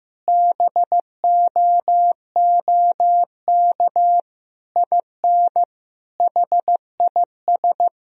Morse code, heard as a keyed tone, about 15 wpm, 700 Hz. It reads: BOOK IN HIS
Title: Sentences from Top 1000 Words - Review Part 3 - 15wpm